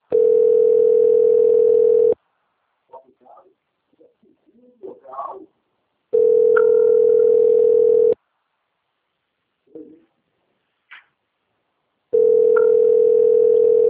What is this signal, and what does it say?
Ringback tone (North America), 440 + 480 Hz, with speech.
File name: rg-1600-unknown- - - .14817